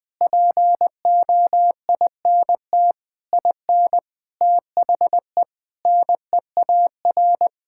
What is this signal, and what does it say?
Morse code, heard as a keyed tone, about 21 wpm, 700 Hz. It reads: POINT IN THE NEAR
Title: Sentences with 7-Word Phrases - 20wpm